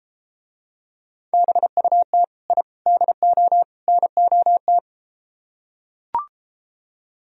Morse code, heard as a keyed tone, about 33 wpm, 700 Hz. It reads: BUT I DO NOT